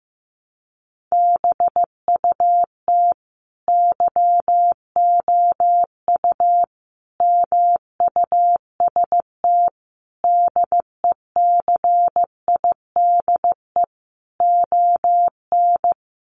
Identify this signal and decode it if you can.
Morse code, heard as a keyed tone, about 15 wpm, 700 Hz. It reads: BUT YOU MUST DECIDE ON
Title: Sentences from Top 600 Words 15wpm